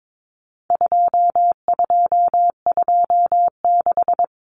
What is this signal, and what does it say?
Morse code, heard as a keyed tone, about 22 wpm, 700 Hz. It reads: 2226
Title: Common Cut Numbers - 4-Digit 22wpm